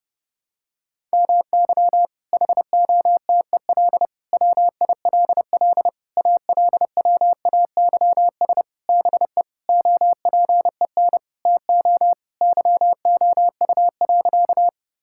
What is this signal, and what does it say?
Morse code, heard as a keyed tone, about 30 wpm, 700 Hz. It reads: MY HOTEL WILL ALWAYS BE OPEN TO YOU.
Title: Sentences from Top 1600 Words 30wpm